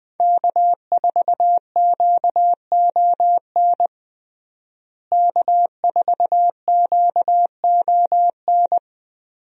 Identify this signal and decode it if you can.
Morse code, heard as a keyed tone, about 20 wpm, 700 Hz. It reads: K4QON K4QON
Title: USA Call Signs - 5 Characters - Repeated 3x - 20wpm